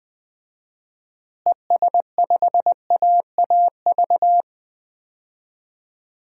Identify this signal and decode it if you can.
Morse code, heard as a keyed tone, about 20 wpm, 700 Hz. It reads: ES5AAV
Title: DXCC-100 Call Signs - 6 Characters - Repeated 3x - 20wpm